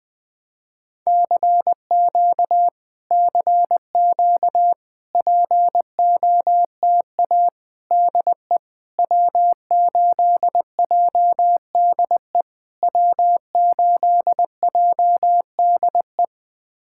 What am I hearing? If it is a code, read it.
Morse code, heard as a keyed tone, about 20 wpm, 700 Hz. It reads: CQ CQ POTA DE W8JDE W8JDE
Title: POTA QSO Practice - 20wpm